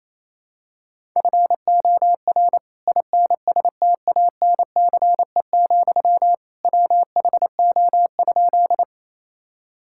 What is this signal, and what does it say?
Morse code, heard as a keyed tone, about 28 wpm, 700 Hz. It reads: FOR INSTANCE, WHO?
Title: Sentences from Top 1000 Words - Review Part 5 - 28wpm